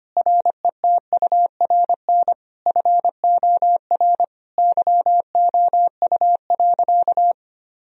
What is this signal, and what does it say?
Morse code, heard as a keyed tone, about 25 wpm, 700 Hz. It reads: RETURN FOR YOU.
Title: Sentences from Top 1000 Words - Review Part 1 - 25wpm